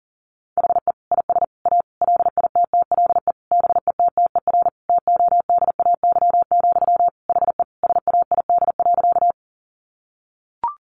Morse code, heard as a keyed tone, about 40 wpm, 700 Hz. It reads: HE IS A LITTLE BETTER TODAY, HE SAID.
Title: Sentences from Top 600 Words 40wpm